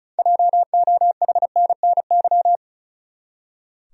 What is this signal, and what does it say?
Morse code, heard as a keyed tone, about 36 wpm, 700 Hz. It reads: JOHNNY
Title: USA Common Names - 35wpm